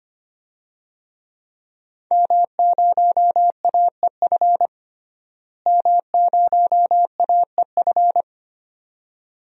Morse code, heard as a keyed tone, about 25 wpm, 700 Hz. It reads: M0AEF M0AEF
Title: DXCC-100 Call Signs - 5 Characters - Repeated 2x - 25wpm